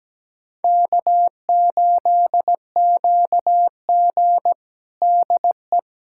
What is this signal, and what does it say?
Morse code, heard as a keyed tone, about 17 wpm, 700 Hz. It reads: K8QG DE
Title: USA Callsign DE Callsign - 4 Characters - 17wpm